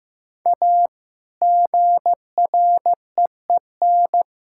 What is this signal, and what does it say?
Morse code, heard as a keyed tone, about 15 wpm, 700 Hz. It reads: A GREEN
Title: Sentences from Top 1600 Words 15wpm